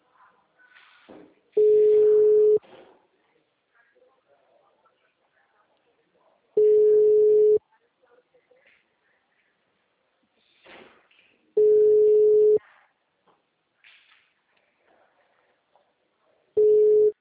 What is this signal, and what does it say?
Ringback tone (Europe, 425 Hz), with speech.